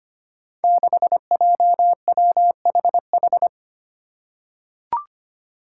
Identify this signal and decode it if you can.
Morse code, heard as a keyed tone, about 25 wpm, 700 Hz. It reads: 6JWHH
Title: Five Challenging Letter-Number Combinations 25wpm